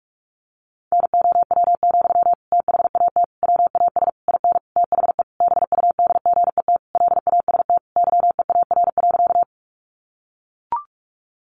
Morse code, heard as a keyed tone, about 44 wpm, 700 Hz. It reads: NOW, THAT WAS IN THE BUDGET LAST YEAR.
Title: Sentences from Top 1200 Words 45wpm